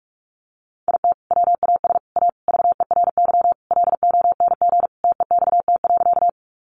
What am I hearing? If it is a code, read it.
Morse code, heard as a keyed tone, about 44 wpm, 700 Hz. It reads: IT WAS A VERY LONG TEXT.
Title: Sentences from Top 1200 Words 45wpm